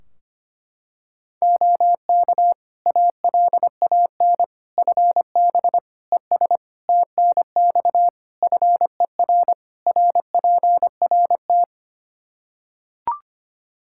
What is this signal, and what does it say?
Morse code, heard as a keyed tone, about 25 wpm, 700 Hz. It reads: OK ALAN FB ES TNX FER RPRT
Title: Practice QSO - Exchange 2 - Info - 25wpm